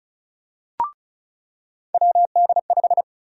Morse code, heard as a keyed tone, about 36 wpm, 700 Hz. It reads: WD5
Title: USA Call Sign Prefixes - 35wpm